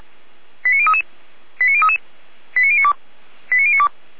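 Five-tone selective call (ZVEI), heard as quick runs of five tones, tones of about 70 ms. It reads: 89033 89033 89092 89092
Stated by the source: Five-tone ZVEI selective call, address 89501, received on an SDR.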